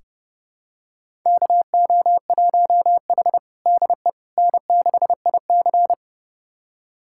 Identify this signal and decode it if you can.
Morse code, heard as a keyed tone, about 30 wpm, 700 Hz. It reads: KO1H DE N6IC
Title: USA Callsign DE Callsign - 4 Characters - 30wpm